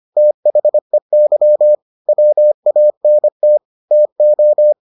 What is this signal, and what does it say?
Morse code, heard as a keyed tone, about 25 wpm, 600 Hz. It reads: THEY WANT TO